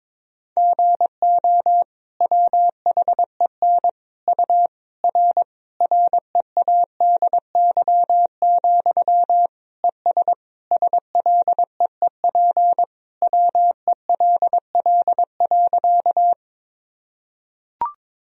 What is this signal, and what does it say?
Morse code, heard as a keyed tone, about 22 wpm, 700 Hz. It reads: GO WHEN U R READY, ES SLEEP WELL.
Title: Sentences from Top 500 Words and Abbreviations 22wpm